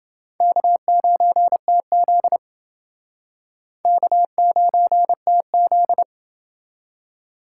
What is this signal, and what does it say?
Morse code, heard as a keyed tone, about 29 wpm, 700 Hz. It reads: K9TZ K9TZ